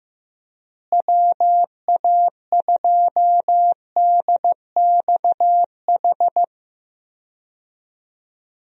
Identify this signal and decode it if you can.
Morse code, heard as a keyed tone, about 15 wpm, 700 Hz. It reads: WA2DXH
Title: USA Call Signs - 6 Characters - Repeated 2x - 15wpm